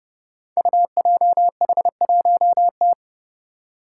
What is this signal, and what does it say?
Morse code, heard as a keyed tone, about 30 wpm, 700 Hz. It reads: UJH1T
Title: Five Challenging Letter-Number Combinations 30wpm